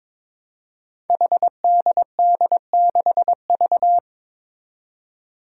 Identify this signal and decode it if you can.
Morse code, heard as a keyed tone, about 22 wpm, 700 Hz. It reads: HDD6V
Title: Five Challenging Letter-Number Combinations 22wpm